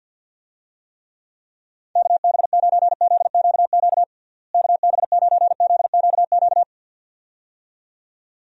Morse code, heard as a keyed tone, about 50 wpm, 700 Hz. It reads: KB9ZQQ KB9ZQQ
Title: USA Call Signs - 6 Characters - Repeated 2x - 50wpm